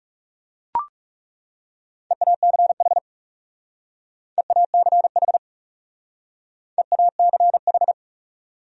Morse code, heard as a keyed tone, about 37 wpm, 700 Hz. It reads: EACH EACH EACH
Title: Speed-Racing Top 300 Words 30wpm